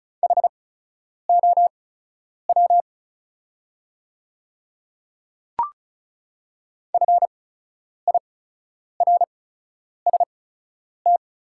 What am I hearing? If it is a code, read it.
Morse code, heard as a keyed tone, about 36 wpm, spaced as for 10 wpm, 690 Hz. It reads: HOW FIRST
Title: 8x Character Spacing - Top 200 Words - 35wpm